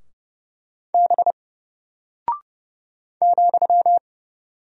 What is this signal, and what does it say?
Morse code, heard as a keyed tone, about 30 wpm, 700 Hz. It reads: B ,
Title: Single Character (Letter:Number:Punctuation:Prosign) 30wpm